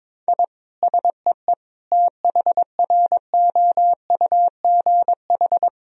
Morse code, heard as a keyed tone, about 21 wpm, 700 Hz. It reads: I SEE THROUGH
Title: Sentences from Top 1000 Words - Review Part 1 - 22wpm